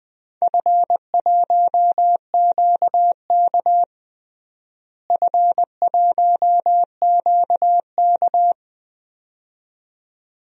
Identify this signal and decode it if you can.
Morse code, heard as a keyed tone, about 20 wpm, 700 Hz. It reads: F1QK F1QK
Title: DXCC-100 Call Signs - 4 Characters - Repeated 2x - 20wpm